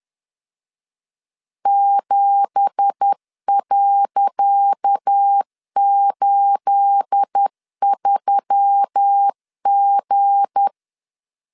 Morse code, heard as a keyed tone, about 10 wpm, 800 Hz. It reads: 7.83G
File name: LEZIONI / 10 SIGNS / 10C DOT 10wpm 2940